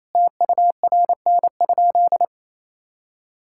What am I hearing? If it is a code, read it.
Morse code, heard as a keyed tone, about 28 wpm, 700 Hz. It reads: TURN?
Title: Sentences from Top 1600 Words 28wpm